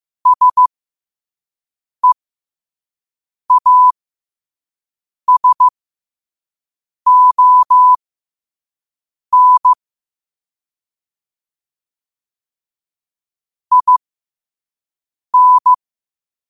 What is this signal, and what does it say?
Morse code, heard as a keyed tone, about 15 wpm, spaced as for 5 wpm, 1000 Hz. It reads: SEASON IN
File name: qotdfiles / 5wpm